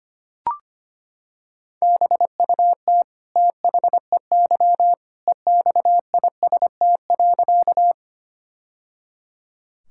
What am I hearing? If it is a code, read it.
Morse code, heard as a keyed tone, about 25 wpm, 700 Hz. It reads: BUT THEY EXIST.
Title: Sentences from Top 1000 Words - Review Part 5 - 25wpm